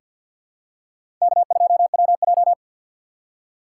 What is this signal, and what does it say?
Morse code, heard as a keyed tone, about 48 wpm, 700 Hz. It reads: K2WJ
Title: USA Call Signs - 4 Characters - 3x Character Spacing - 50wpm